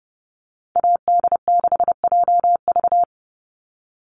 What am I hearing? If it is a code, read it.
Morse code, heard as a keyed tone, about 29 wpm, 700 Hz. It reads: AD6JV
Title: Call Signs Repeated 3x - 30wpm